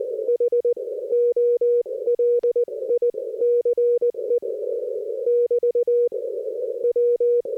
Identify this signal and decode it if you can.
Morse code, heard as a keyed tone, about 19 wpm, spaced as for 17 wpm, 480 Hz. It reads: H OLICE <BT> W
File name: 60m qso